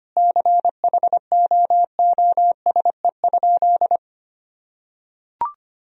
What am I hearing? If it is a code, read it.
Morse code, heard as a keyed tone, about 25 wpm, 700 Hz. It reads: CHOOSE?
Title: Sentences from Top 1000 Words - Review Part 4 - 25wpm